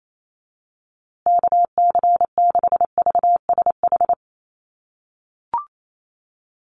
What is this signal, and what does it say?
Morse code, heard as a keyed tone, about 28 wpm, 700 Hz. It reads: KC6VSH
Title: USA Call Signs Repeated 2x 28wpm